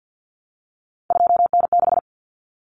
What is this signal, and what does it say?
Morse code, heard as a keyed tone, about 50 wpm, 700 Hz. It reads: 2N6
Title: Common Cut Numbers - 3-Digit 50wpm